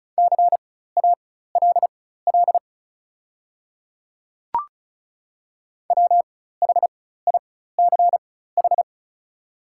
Morse code, heard as a keyed tone, about 36 wpm, spaced as for 16 wpm, 700 Hz. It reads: CALL WHICH
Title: 4x Character Spacing - Top 100 Words - 35wpm